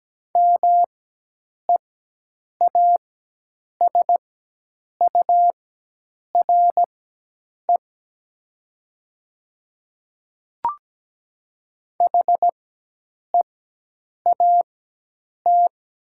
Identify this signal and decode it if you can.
Morse code, heard as a keyed tone, about 17 wpm, spaced as for 8 wpm, 700 Hz. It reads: MEASURE HEAT
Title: 4x Character Spacing - Top 500 Words - 17wpm